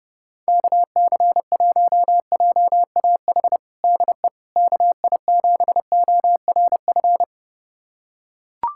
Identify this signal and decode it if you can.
Morse code, heard as a keyed tone, about 30 wpm, 700 Hz. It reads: KC1JAH DE KI7ORF
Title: USA Callsign DE Callsign 30wpm